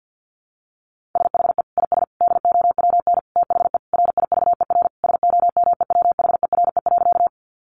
Morse code, heard as a keyed tone, about 50 wpm, 700 Hz. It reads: SHE IS DOWN THE RIVER SOMEWHERE.
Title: Sentences from Top 1400 Words 50wpm